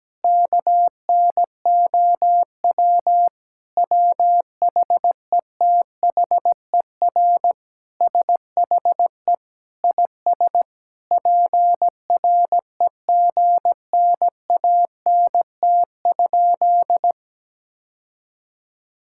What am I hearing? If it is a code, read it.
Morse code, heard as a keyed tone, about 17 wpm, 700 Hz. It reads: KNOW WHETHER SHE IS PREGNANT?
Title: Sentences from Top 1000 Words - Review Part 6 - 17wpm